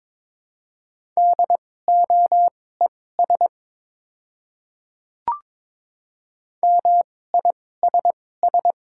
Morse code, heard as a keyed tone, about 22 wpm, spaced as for 16 wpm, 700 Hz. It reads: DOES MISS
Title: 2x Character Spacing - Top 500 Words - 22wpm